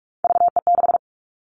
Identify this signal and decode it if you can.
Morse code, heard as a keyed tone, about 44 wpm, 700 Hz. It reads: VE6
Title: Callsign Prefixes for Top 50 DXCC - 45wpm